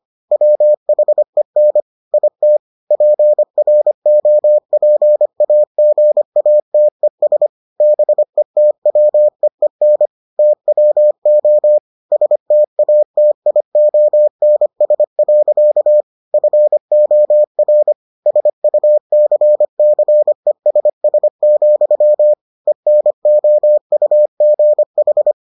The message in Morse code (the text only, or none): WHEN IT PROPAGATES BETWEEN TWO STATIONS. FOR SUCCESS, ENOUGH